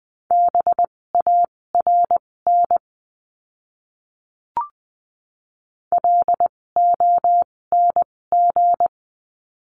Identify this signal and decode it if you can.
Morse code, heard as a keyed tone, about 20 wpm, spaced as for 16 wpm, 700 Hz. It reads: BARN LONG
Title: 256 Lesson 42a - Words 12-20wpm